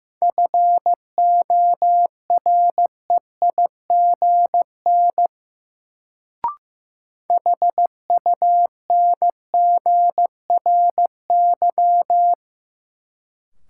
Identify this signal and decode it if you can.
Morse code, heard as a keyed tone, about 15 wpm, 700 Hz. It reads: FOREIGN HUNGRY
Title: Sentences from Top 1400 Words - New Word Drill - 15wpm